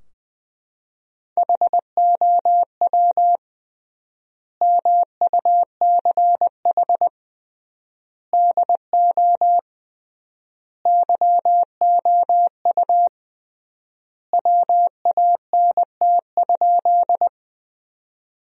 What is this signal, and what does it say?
Morse code, heard as a keyed tone, about 20 wpm, 700 Hz. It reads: HOW MUCH DO YOU WANT?
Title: 3x Word-Spacing - Sentences from Top 200 Words - 20wpm